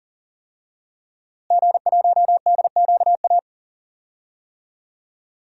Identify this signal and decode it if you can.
Morse code, heard as a keyed tone, about 40 wpm, 700 Hz. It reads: G1DQA
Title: DXCC-100 Call Signs - 5 Characters - Repeated 2x - 40wpm